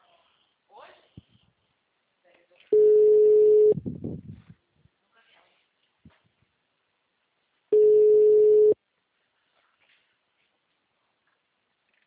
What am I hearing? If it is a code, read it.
Ringback tone (Europe, 425 Hz), with speech.